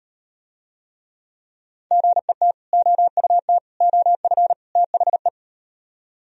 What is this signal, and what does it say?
Morse code, heard as a keyed tone, about 38 wpm, 700 Hz. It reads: GET OUT OF THE